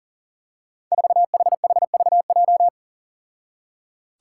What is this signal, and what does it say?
Morse code, heard as a keyed tone, about 37 wpm, 700 Hz. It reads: 4HHVJ
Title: Five Challenging Letter-Number Combinations 40wpm